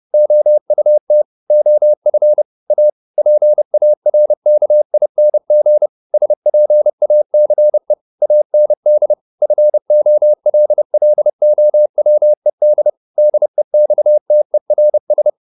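Morse code, heard as a keyed tone, about 30 wpm, 600 Hz. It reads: OUT OF A PARKING SPACE AND FOLLOWED DEXTERS
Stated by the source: Morse code at 30 wpm, the novel Never by Ken Follett in English, from CW med Gnister.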